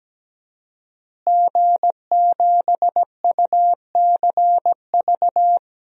Morse code, heard as a keyed tone, about 17 wpm, 700 Hz. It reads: G7UCV
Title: DXCC-100 Call Signs - 5 Characters - Repeated 2x - 17wpm